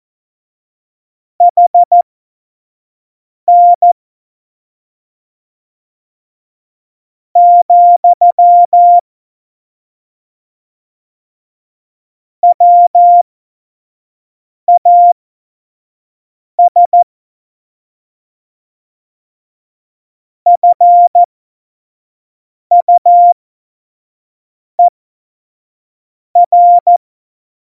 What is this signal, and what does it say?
Morse code, heard as a keyed tone, about 14 wpm, spaced as for 4 wpm, 700 Hz. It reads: HN, WAS FUER
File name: Morsekurs / Witze / Witz 08. Mai 2018, 25 F70